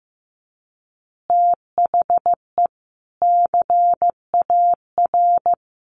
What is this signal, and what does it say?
Morse code, heard as a keyed tone, about 14 wpm, 700 Hz. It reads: THE CAR
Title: Easy Phrases 15wpm